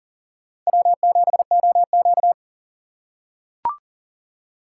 Morse code, heard as a keyed tone, about 40 wpm, 700 Hz. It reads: W7OQ
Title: K1USN SST QSO Practice - 40wpm